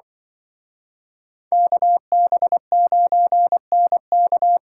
Morse code, heard as a keyed tone, about 24 wpm, 700 Hz. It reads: KB9NK
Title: Speed Racing - USA 2x2 Call Signs 20wpm